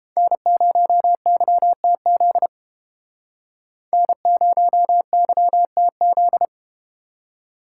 Morse code, heard as a keyed tone, about 32 wpm, 700 Hz. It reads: N0YTZ N0YTZ